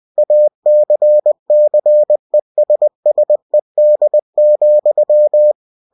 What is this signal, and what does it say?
Morse code, heard as a keyed tone, about 20 wpm, 600 Hz. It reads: ACCESSED,
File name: cw-books / winnie-the-pooh / winnie-20wpm-page-0115